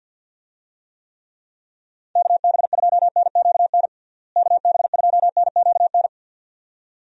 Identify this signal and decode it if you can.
Morse code, heard as a keyed tone, about 51 wpm, 700 Hz. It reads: KB2NQN KB2NQN
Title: USA Call Signs - 6 Characters - Repeated 2x - 50wpm